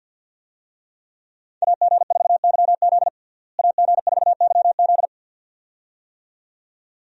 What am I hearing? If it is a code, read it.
Morse code, heard as a keyed tone, about 48 wpm, 700 Hz. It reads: AG4YZ AG4YZ
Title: USA Call Signs - 5 Characters - Repeated 2x - 50wpm